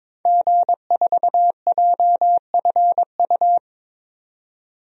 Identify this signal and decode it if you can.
Morse code, heard as a keyed tone, about 22 wpm, 700 Hz. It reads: G4JFU
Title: DXCC-100 Call Signs - 5 Characters - 22wpm